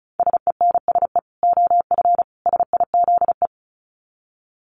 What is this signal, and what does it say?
Morse code, heard as a keyed tone, about 34 wpm, 700 Hz. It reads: SENSE OF SIZE
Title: Dit Challenge 35wpm